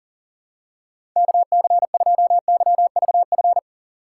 Morse code, heard as a keyed tone, about 37 wpm, 700 Hz. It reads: KC2YVF
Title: USA Call Signs - 6 Characters - 4x Character Spacing - 40wpm